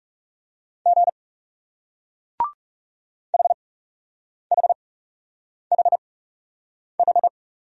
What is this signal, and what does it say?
Morse code, heard as a keyed tone, about 40 wpm, spaced as for 8 wpm, 700 Hz. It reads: G HHHH